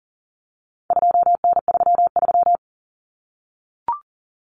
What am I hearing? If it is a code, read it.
Morse code, heard as a keyed tone, about 40 wpm, 700 Hz. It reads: 2N33
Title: Common Cut Numbers - 4-Digit 40wpm